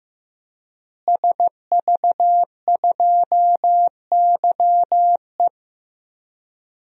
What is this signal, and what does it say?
Morse code, heard as a keyed tone, about 15 wpm, 700 Hz. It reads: SV2YE
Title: DXCC-100 Call Signs - 5 Characters - Repeated 3x - 15wpm